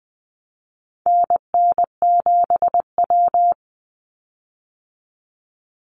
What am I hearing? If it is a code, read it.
Morse code, heard as a keyed tone, about 20 wpm, 700 Hz. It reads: NN7W
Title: Call Signs Repeated 3x - 20wpm